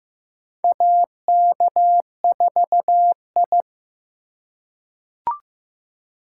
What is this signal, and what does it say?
Morse code, heard as a keyed tone, about 15 wpm, 700 Hz. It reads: AK4I
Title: USA Call Signs - 4 Characters - 4x Character Spacing - 15wpm